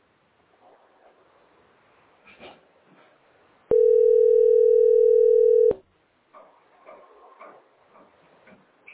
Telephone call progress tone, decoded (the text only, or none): ringback tone